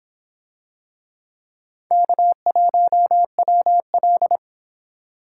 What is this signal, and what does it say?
Morse code, heard as a keyed tone, about 26 wpm, 700 Hz. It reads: K1WL